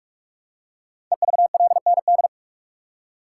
Morse code, heard as a keyed tone, about 44 wpm, 700 Hz. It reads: EVLND
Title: Code Groups - 5 Letter:Numbers - 45wpm